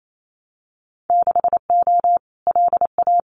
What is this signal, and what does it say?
Morse code, heard as a keyed tone, about 28 wpm, 700 Hz. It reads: 6O LA
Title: Winter Field Day Exchanges Encore 28wpm